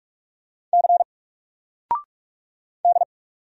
Morse code, heard as a keyed tone, about 46 wpm, 700 Hz. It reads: C D
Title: Single Letter-Number 45wpm